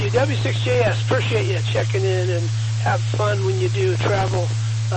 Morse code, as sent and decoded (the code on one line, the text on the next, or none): none